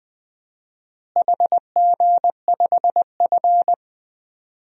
Morse code, heard as a keyed tone, about 20 wpm, 700 Hz. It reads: HG5F
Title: DXCC-100 Call Signs - 4 Characters - Repeated 3x - 20wpm